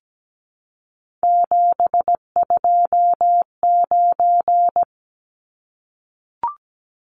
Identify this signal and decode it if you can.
Morse code, heard as a keyed tone, about 17 wpm, 700 Hz. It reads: 729
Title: Numbers - 3-Digit 17wpm